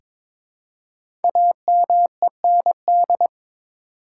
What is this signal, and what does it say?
Morse code, heard as a keyed tone, about 23 wpm, 700 Hz. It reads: AMEND
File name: 5 Letter Words - First 3 Letters Given - 22wpm